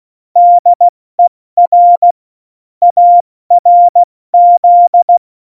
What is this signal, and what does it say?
Morse code, heard as a keyed tone, about 16 wpm, 700 Hz. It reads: DER ARZ
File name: Morsekurs / Witze / Witz 17 April 2018, 70 F80